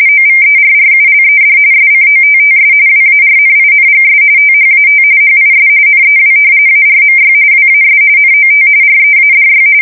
RTTY, about 45.45 baud. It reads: OVERRULED SCOTTS WARNINGS THAT HIS NEW ARMY WAS NOT READY
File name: fsk170 / usnavy3